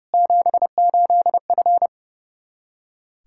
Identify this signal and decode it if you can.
Morse code, heard as a keyed tone, about 30 wpm, 700 Hz. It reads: 78F
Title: Practice QSO - Exchange 2 - Info - 30wpm